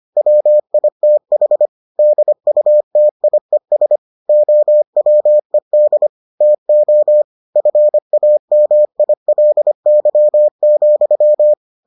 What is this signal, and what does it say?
Morse code, heard as a keyed tone, about 25 wpm, 600 Hz. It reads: WITH DUTIES OWED TO FAMILY,